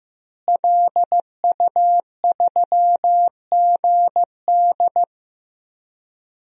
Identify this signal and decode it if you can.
Morse code, heard as a keyed tone, about 15 wpm, 700 Hz. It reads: LU3GD
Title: DXCC-100 Call Signs - 5 Characters - Repeated 2x - 15wpm